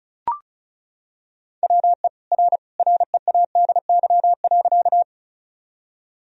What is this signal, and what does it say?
Morse code, heard as a keyed tone, about 36 wpm, 700 Hz. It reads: WE R READY.
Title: Sentences from Top 400 Words and Abbreviations 35wpm